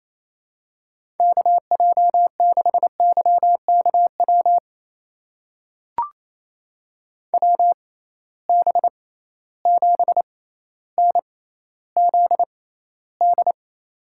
Morse code, heard as a keyed tone, about 28 wpm, spaced as for 14 wpm, 700 Hz. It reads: KJ6YKW WB7NZD
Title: USA Call Signs - 6 Characters - 6x Character Spacing - 28wpm